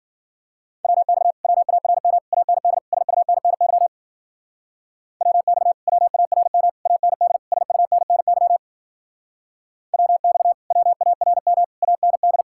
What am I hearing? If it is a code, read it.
Morse code, heard as a keyed tone, about 52 wpm, 700 Hz. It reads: WX WARM AND SUNNY WX WARM AND SUNNY WX WARM AND